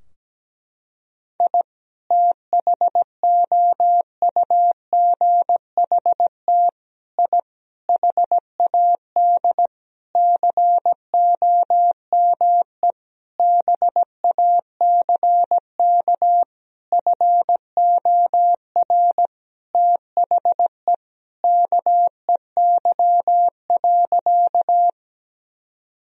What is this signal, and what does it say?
Morse code, heard as a keyed tone, about 17 wpm, 700 Hz. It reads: I THOUGHT I HAD COME BACK FOR THE KEY.
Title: Sentences from Top 1000 Words - Review Part 4 - 17wpm